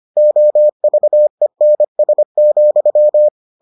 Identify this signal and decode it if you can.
Morse code, heard as a keyed tone, about 25 wpm, 600 Hz. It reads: OVENS,